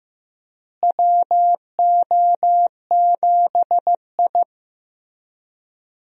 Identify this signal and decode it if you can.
Morse code, heard as a keyed tone, about 15 wpm, 700 Hz. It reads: WO7I
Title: USA Call Signs - 4 Characters - Repeated 3x - 15wpm